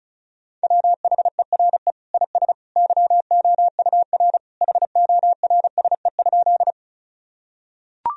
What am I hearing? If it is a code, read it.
Morse code, heard as a keyed tone, about 36 wpm, 700 Hz. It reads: WHERE IS YOUR HORSE?
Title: Sentences from Top 1000 Words - Review Part 1 - 35wpm